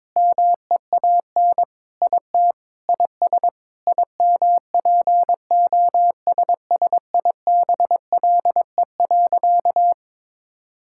Morse code, heard as a keyed tone, about 22 wpm, 700 Hz. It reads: MEAN IT IS IMPOSSIBLE.
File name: practice / 22wpm / Sentences with 4-Word Phrases - 22wpm